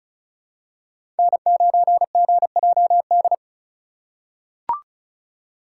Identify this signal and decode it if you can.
Morse code, heard as a keyed tone, about 34 wpm, 700 Hz. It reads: N9GJD
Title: USA Call Signs - 5 Characters - 4x Character Spacing - 35wpm